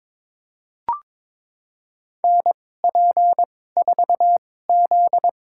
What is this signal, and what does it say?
Morse code, heard as a keyed tone, about 21 wpm, spaced as for 16 wpm, 700 Hz. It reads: NP4Z
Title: USA Call Signs - 2x Character Spacing - 22wpm